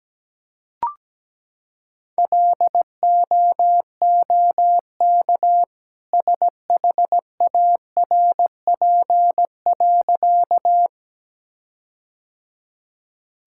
Morse code, heard as a keyed tone, about 17 wpm, 700 Hz. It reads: LOOK SHARP.
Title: Sentences from Top 1600 Words 17wpm